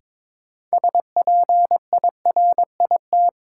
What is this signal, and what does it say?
Morse code, heard as a keyed tone, about 21 wpm, 700 Hz. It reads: SPIRIT
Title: Sentences from Top 1400 Words - New Word Drill - 22wpm